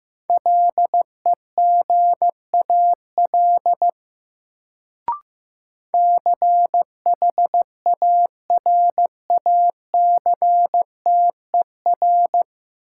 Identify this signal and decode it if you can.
Morse code, heard as a keyed tone, about 15 wpm, 700 Hz. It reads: LEGAL CHARACTER
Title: Sentences from Top 1200 Words - New Word Drill Alt - 15wpm